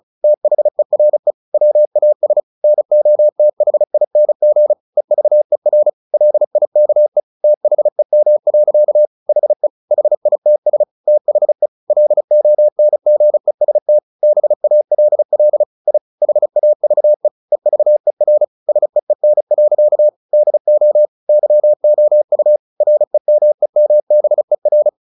Morse code, heard as a keyed tone, about 34 wpm, 600 Hz. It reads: THERE WAS NOTHING EVER LIKE THEM. HE HITS THE LONGEST BALL I HAVE EVER SEEN. DO YOU REMEMBER
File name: the old man and the sea / the old man and the sea 1 35wpm